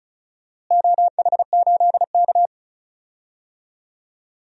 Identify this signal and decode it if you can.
Morse code, heard as a keyed tone, about 36 wpm, 700 Hz. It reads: OH8K
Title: DXCC-100 Call Signs - 4 Characters - Repeated 2x - 35wpm